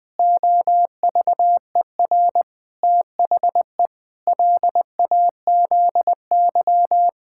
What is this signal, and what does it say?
Morse code, heard as a keyed tone, about 20 wpm, 700 Hz. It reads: OVER THE LAZY
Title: Pangram Challenge v2 - 20wpm